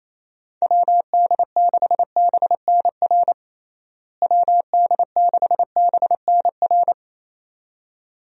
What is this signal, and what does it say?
Morse code, heard as a keyed tone, about 28 wpm, 700 Hz. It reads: WD6BNR WD6BNR
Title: USA Call Signs - 6 Characters - Repeated 2x - 28wpm